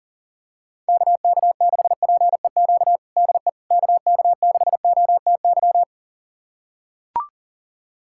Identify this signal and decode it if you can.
Morse code, heard as a keyed tone, about 40 wpm, 700 Hz. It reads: KK6PEQ DE KK6OTY
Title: USA Callsign DE Callsign - 6 Characters - 40wpm